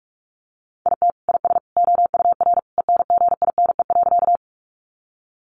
Morse code, heard as a keyed tone, about 45 wpm, 700 Hz. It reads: IT IS OUR ENGINE.
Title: Sentences from Top 1500 Words 45wpm